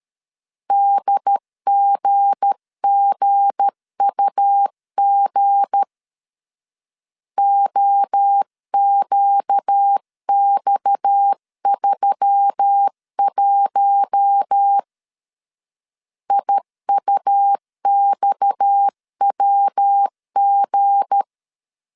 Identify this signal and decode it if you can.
Morse code, heard as a keyed tone, about 13 wpm, 800 Hz. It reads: DGGUG OQX31 IUXWG
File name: LEZIONI / 7 GF9Z / 7 G 12wpm 2055